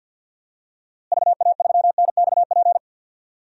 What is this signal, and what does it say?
Morse code, heard as a keyed tone, about 53 wpm, 700 Hz. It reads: VA3NXP
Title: DXCC-100 Call Signs - 6 Characters - Repeated 2x - 50wpm